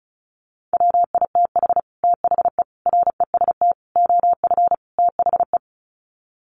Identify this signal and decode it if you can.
Morse code, heard as a keyed tone, about 34 wpm, 700 Hz. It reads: WITH THE REST OF THE
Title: Sets of 5 Words 35wpm